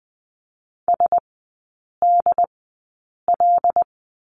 Morse code, heard as a keyed tone, about 20 wpm, spaced as for 8 wpm, 700 Hz. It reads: SDL
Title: 121 Lesson 20e - Number 3 - Three Characters 20wpm